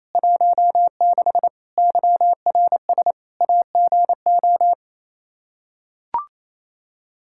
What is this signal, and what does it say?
Morse code, heard as a keyed tone, about 28 wpm, 700 Hz. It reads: 16 YRS AGO
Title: Practice QSO - Exchange 3 - More Info - 28wpm